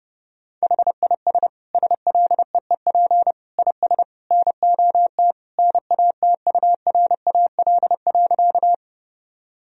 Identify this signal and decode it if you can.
Morse code, heard as a keyed tone, about 30 wpm, 700 Hz. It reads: HIS SLEEP IS NOT NATURAL.
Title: Sentences from Top 1000 Words - Review Part 4 - 30wpm